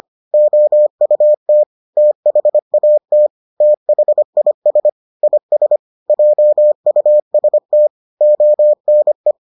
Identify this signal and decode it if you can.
Morse code, heard as a keyed tone, about 25 wpm, 600 Hz. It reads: OUT THAT THIS IS JUST ONE